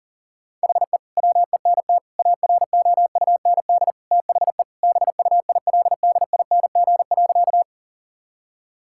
Morse code, heard as a keyed tone, about 40 wpm, 700 Hz. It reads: HE WENT AROUND THE BUILDING.
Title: Sentences from Top 1000 Words - Review Part 5 - 40wpm